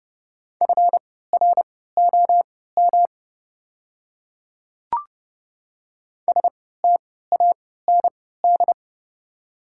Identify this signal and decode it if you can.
Morse code, heard as a keyed tone, about 31 wpm, spaced as for 17 wpm, 690 Hz. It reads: FROM STAND
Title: 3x Character Spacing - Top 300 Words - 30wpm